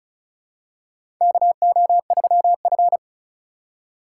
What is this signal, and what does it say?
Morse code, heard as a keyed tone, about 36 wpm, 700 Hz. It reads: KO3F
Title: USA Call Signs - 4 Characters - 3x Character Spacing - 35wpm